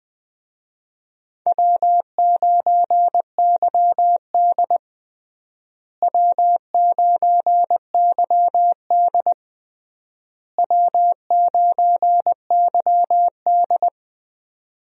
Morse code, heard as a keyed tone, about 20 wpm, 700 Hz. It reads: W9YD W9YD W9YD
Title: USA Call Signs - 4 Characters - Repeated 3x - 20wpm